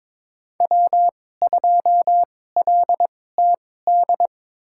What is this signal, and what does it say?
Morse code, heard as a keyed tone, about 22 wpm, spaced as for 16 wpm, 700 Hz. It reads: W2LTD
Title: USA Call Signs - 5 Characters - 2x Character Spacing - 22wpm